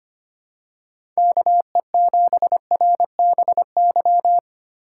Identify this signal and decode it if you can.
Morse code, heard as a keyed tone, about 25 wpm, 700 Hz. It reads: KE7RBY
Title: USA Call Signs - 6 Characters - 2x Character Spacing - 25wpm